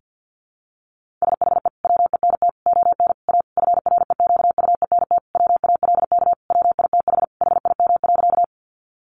Morse code, heard as a keyed tone, about 50 wpm, 700 Hz. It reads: SHE WENT ON A FREQUENT WALK WITH HIM.
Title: Sentences from Top 1400 Words 50wpm